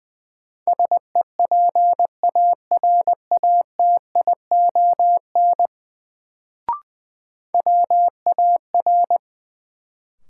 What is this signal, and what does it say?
Morse code, heard as a keyed tone, about 20 wpm, 700 Hz. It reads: SEPARATION WAR
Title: Sets of 5 Words - New Word Drill - 20wpm